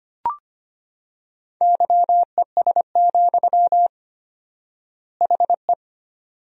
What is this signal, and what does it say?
Morse code, heard as a keyed tone, about 25 wpm, 700 Hz. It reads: YES, HE
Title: 4x Word-Spacing - Sentences from Top 500 Words - 25wpm